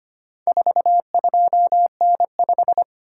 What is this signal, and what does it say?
Morse code, heard as a keyed tone, about 25 wpm, 700 Hz. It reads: 42N5
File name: practice / 25wpm / Four Letter-Number Combinations - 25wpm